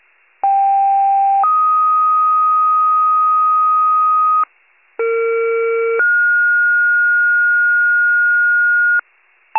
Two-tone page, heard as a one-second tone, then a three-second tone, about 790 then 1250 Hz, about 460 then 1430 Hz, with noise. It is heard from 0.4 to 9.0 s.
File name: pager / 21 FIRE / wcalert